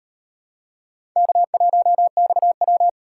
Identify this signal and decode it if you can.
Morse code, heard as a keyed tone, about 38 wpm, 700 Hz. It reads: K1XW